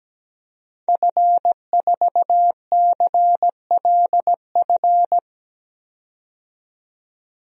Morse code, heard as a keyed tone, about 17 wpm, 700 Hz. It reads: F4CLF
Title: DXCC-100 Call Signs - 5 Characters - 17wpm